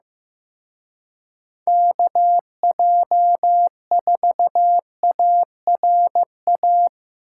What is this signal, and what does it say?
Morse code, heard as a keyed tone, about 18 wpm, 700 Hz. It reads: KJ4ARA
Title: Speed Racing - USA 2x3 Call Signs 15wpm